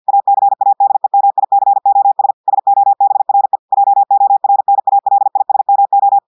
Morse code, heard as a keyed tone, about 50 wpm, 800 Hz. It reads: ACADEMICOS SOBRE JORNALISMO
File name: audios / portugais / portugais texte en clair 1 vitesse 50 mots minutes